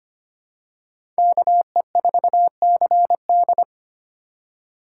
Morse code, heard as a keyed tone, about 25 wpm, 700 Hz. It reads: KE4CD
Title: USA Call Signs - 5 Characters - 4x Character Spacing - 25wpm